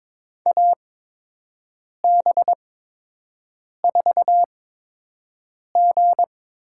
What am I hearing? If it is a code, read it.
Morse code, heard as a keyed tone, about 22 wpm, spaced as for 6 wpm, 700 Hz. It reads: AB4G
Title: USA Call Signs - 5 Characters - 8x Character Spacing - 22wpm